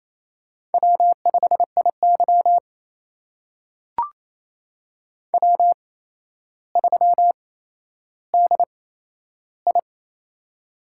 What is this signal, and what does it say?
Morse code, heard as a keyed tone, about 28 wpm, spaced as for 12 wpm, 700 Hz. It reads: W5IY W3DI
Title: USA Call Signs - 4 Characters - 8x Character Spacing - 28wpm